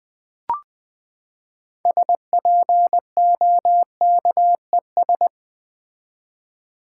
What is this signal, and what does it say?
Morse code, heard as a keyed tone, about 20 wpm, 700 Hz. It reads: SPOKES
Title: Build A Word - 6 Letters At A Time - 12-Letter Words - 20wpm